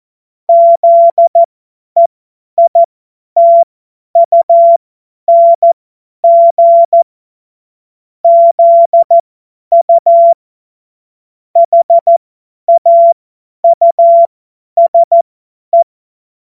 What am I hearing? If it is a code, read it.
Morse code, heard as a keyed tone, about 14 wpm, spaced as for 10 wpm, 680 Hz. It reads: ZEITUNG ZU HAUSE
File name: Morsekurs / Witze / Witz 16. Mai 2023 50 F70